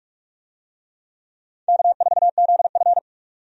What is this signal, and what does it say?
Morse code, heard as a keyed tone, about 46 wpm, 700 Hz. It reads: K4ZF
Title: USA Call Signs - 4 Characters - 45wpm